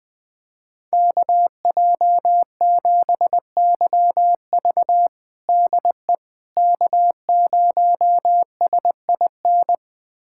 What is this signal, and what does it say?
Morse code, heard as a keyed tone, about 20 wpm, 700 Hz. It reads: KJ7YV DE K0SIN
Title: USA Callsign DE Callsign - 5 Characters - 20wpm